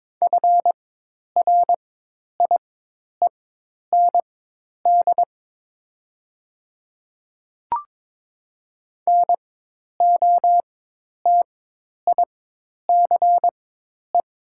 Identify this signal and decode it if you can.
Morse code, heard as a keyed tone, about 22 wpm, spaced as for 10 wpm, 700 Hz. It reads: FRIEND NOTICE
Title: 4x Character Spacing - Top 500 Words - 22wpm